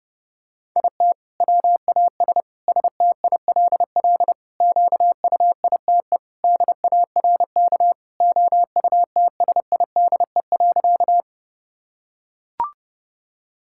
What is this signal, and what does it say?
Morse code, heard as a keyed tone, about 30 wpm, 700 Hz. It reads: IT WAS STILL QUITE DARK OUTSIDE.
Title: Sentences from Top 1000 Words - Review Part 6 - 30wpm